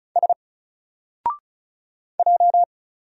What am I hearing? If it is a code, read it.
Morse code, heard as a keyed tone, about 36 wpm, 700 Hz. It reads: S J
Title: Single Character (Letter:Number:Punctuation:Prosign) 35wpm